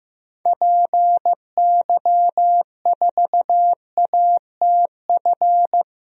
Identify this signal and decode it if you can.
Morse code, heard as a keyed tone, about 15 wpm, 700 Hz. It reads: PY4ATF
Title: DXCC-100 Call Signs - 6 Characters - 15wpm